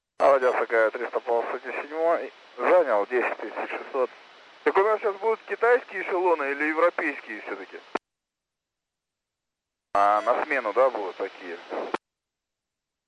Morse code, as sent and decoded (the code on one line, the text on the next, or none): none